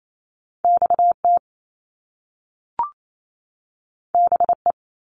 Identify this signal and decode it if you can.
Morse code, heard as a keyed tone, about 28 wpm, 700 Hz. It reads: XT BE
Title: Top Two Letter Combinations in Words 28wpm